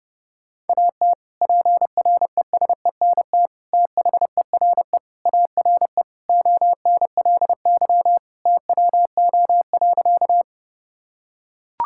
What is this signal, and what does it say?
Morse code, heard as a keyed tone, about 32 wpm, 700 Hz. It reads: AT PRESENT THERE ARE ONLY TWO.
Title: Sentences from Top 1000 Words - Review Part 5 - 30wpm